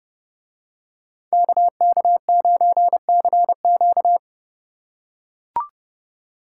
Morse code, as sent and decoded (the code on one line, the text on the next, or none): -.- -.- ----. -.-. --.-
KK9CQ